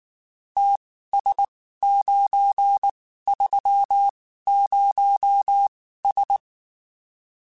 Morse code, heard as a keyed tone, about 19 wpm, spaced as for 14 wpm, 800 Hz. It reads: TS930S
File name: (spaced) / QSOs Spaced-mp3s / kc7sac @18 sp